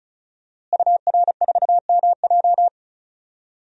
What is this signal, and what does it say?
Morse code, heard as a keyed tone, about 36 wpm, 700 Hz. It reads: UR4MJ
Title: DXCC-100 Call Signs - 5 Characters - Repeated 2x - 35wpm